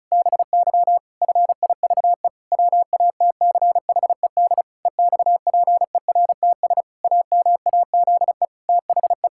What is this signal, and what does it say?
Morse code, heard as a keyed tone, about 34 wpm, 700 Hz. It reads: BY FIVE WATCHED EXPERTS AMAZE THE
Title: Pangram Challenge v2 - 35wpm